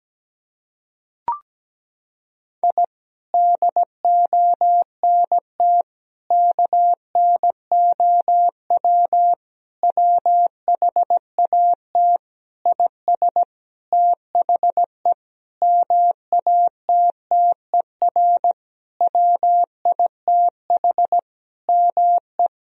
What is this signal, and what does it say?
Morse code, heard as a keyed tone, about 17 wpm, 700 Hz. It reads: I DONT KNOW WHAT IS THE MATTER WITH ME
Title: Sentences from Top 1000 Words - Review Part 4 - 17wpm